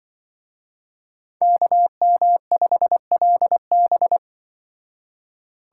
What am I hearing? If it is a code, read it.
Morse code, heard as a keyed tone, about 23 wpm, 700 Hz. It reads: KM5LB
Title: Speed Racing - USA 2x2 Call Signs 20wpm